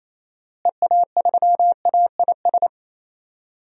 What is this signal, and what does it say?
Morse code, heard as a keyed tone, about 28 wpm, 700 Hz. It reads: EA3AIS
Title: DXCC-100 Call Signs - 6 Characters - Repeated 2x - 28wpm